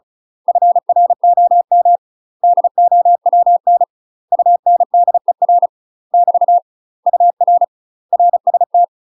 Morse code, heard as a keyed tone, about 34 wpm, 700 Hz. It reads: FROM DOWN UNDER <BT> UR RST
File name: UK MASTER QSOs TRAINING / 35WPM / UK MASTER QSOs 0025